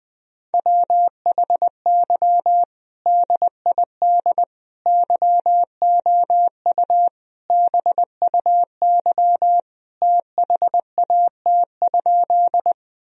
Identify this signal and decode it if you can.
Morse code, heard as a keyed tone, about 21 wpm, 700 Hz. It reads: WHY DID YOU BUY THAT?
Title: Sentences from Top 1000 Words - Review Part 4 - 20wpm